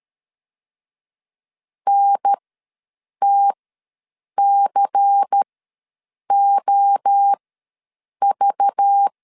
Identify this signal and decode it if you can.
Morse code, heard as a keyed tone, about 13 wpm, spaced as for 7 wpm, 800 Hz. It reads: NTCOV
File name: LEZIONI / 7 GF9Z / 7 GF9 ws 12wpm 3344